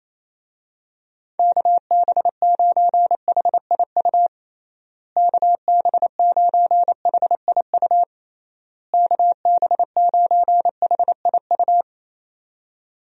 Morse code, heard as a keyed tone, about 28 wpm, 700 Hz. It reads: KB9HIU KB9HIU KB9HIU
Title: USA Call Signs - 6 Characters - Repeated 3x - 28wpm